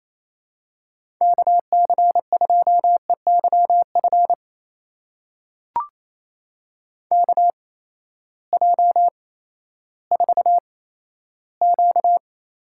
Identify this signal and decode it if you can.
Morse code, heard as a keyed tone, about 28 wpm, spaced as for 14 wpm, 700 Hz. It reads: KC2EYF KJ4Q
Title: USA Call Signs - 6 Characters - 8x Character Spacing - 28wpm